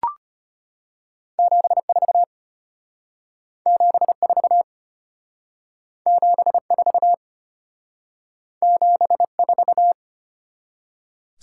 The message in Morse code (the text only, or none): 74 74 74 74